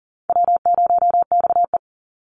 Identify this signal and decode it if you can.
Morse code, heard as a keyed tone, about 40 wpm, 700 Hz. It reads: W0XE
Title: Call Signs - 40wpm